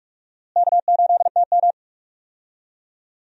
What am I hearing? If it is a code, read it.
Morse code, heard as a keyed tone, about 46 wpm, 700 Hz. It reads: K8TM
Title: USA Call Signs - 4 Characters - Repeated 2x - 45wpm